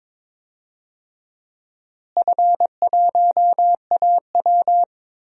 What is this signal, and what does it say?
Morse code, heard as a keyed tone, about 21 wpm, 700 Hz. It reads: F1AW
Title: DXCC-100 Call Signs - 4 Characters - Repeated 2x - 22wpm